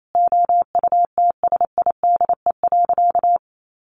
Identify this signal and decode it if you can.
Morse code, heard as a keyed tone, about 28 wpm, 700 Hz. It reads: OUTSIDE.
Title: Sentences from Top 900 Words 28wpm